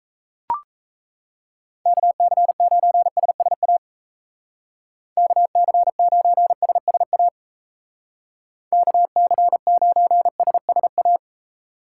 Morse code, heard as a keyed tone, about 33 wpm, 700 Hz. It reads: KC9SSA KC9SSA KC9SSA
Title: Speed Racing - USA Call Signs 28wpm